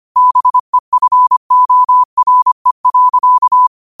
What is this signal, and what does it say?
Morse code, heard as a keyed tone, about 25 wpm, 1000 Hz. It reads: BEFORE.